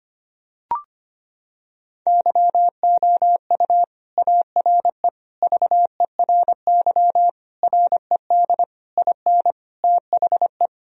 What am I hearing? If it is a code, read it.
Morse code, heard as a keyed tone, about 25 wpm, 700 Hz. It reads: YOU ARE VERY RED IN THE
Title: Sentences from Top 1000 Words - Review Part 2 - 25wpm